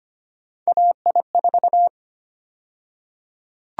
Morse code, heard as a keyed tone, about 25 wpm, 700 Hz. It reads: AI4
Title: USA Call Sign Prefixes - 25wpm